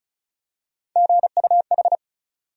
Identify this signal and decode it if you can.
Morse code, heard as a keyed tone, about 34 wpm, 700 Hz. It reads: GUH
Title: USA Call Sign Postfixes - 3 Letters 35wpm